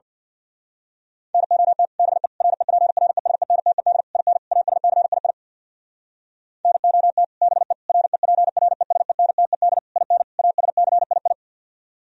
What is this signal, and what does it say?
Morse code, heard as a keyed tone, about 56 wpm, 700 Hz. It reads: NOT BE REPRESENTED IN ASCII NOT BE REPRESENTED IN ASCII
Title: Speed-Racing - Sets of 5 Words 50wpm